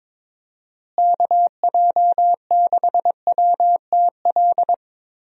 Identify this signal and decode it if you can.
Morse code, heard as a keyed tone, about 22 wpm, 700 Hz. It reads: KJ6WTL
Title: USA Call Signs - 6 Characters - Repeated 3x - 22wpm